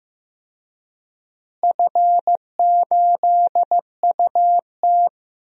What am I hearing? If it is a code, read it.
Morse code, heard as a keyed tone, about 15 wpm, 700 Hz. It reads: F8UT
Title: DXCC-100 Call Signs - 4 Characters - Repeated 3x - 15wpm